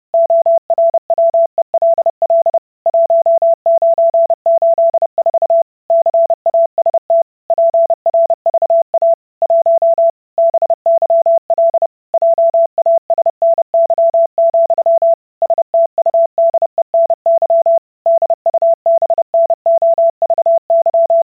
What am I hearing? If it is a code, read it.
Morse code, heard as a keyed tone, about 30 wpm, 670 Hz. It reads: ORWELL 1984 CAST PRVA 1 BYL JASNY, STUDENY DUBNOVY